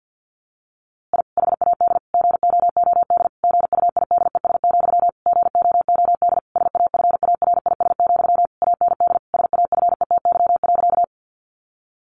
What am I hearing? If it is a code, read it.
Morse code, heard as a keyed tone, about 50 wpm, 700 Hz. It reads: I HAD GOOD GUIDES, GOOD SAFARIS, AND SAFETY.
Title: A Sound Of Thunder 50wpm